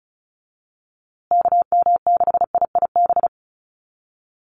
Morse code, heard as a keyed tone, about 34 wpm, 700 Hz. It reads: KM6IIB
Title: USA Call Signs Repeated 2x 35wpm